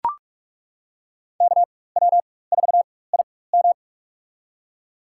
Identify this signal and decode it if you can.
Morse code, heard as a keyed tone, about 42 wpm, spaced as for 21 wpm, 700 Hz. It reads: KW4IM
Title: USA Call Signs - 5 Characters - 4x Character Spacing - 45wpm